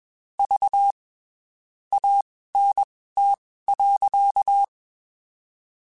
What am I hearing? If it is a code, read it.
Morse code, heard as a keyed tone, about 21 wpm, spaced as for 15 wpm, 800 Hz. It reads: V ANT.
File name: (spaced) / QSOs Spaced-mp3s / WD0FBT 1 @20 sp